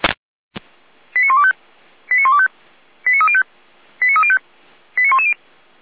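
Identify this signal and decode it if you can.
Five-tone selective call (ZVEI), heard as quick runs of five tones, tones of about 70 ms. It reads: 89125 89125 89305 89305 89110
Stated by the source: ZVEI five-tone selective call received via SDR.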